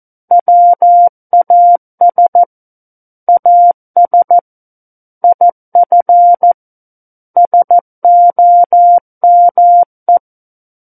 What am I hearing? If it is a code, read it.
Morse code, heard as a keyed tone, about 14 wpm, 700 Hz. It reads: WAS AS IF SOME
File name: Book One / 05-Chapter 5 - 14 WPM